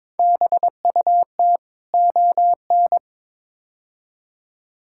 Morse code, heard as a keyed tone, about 21 wpm, 700 Hz. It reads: BUT ON
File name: practice / 22wpm / Build A Phrase - 5-Word Phrases - 22wpm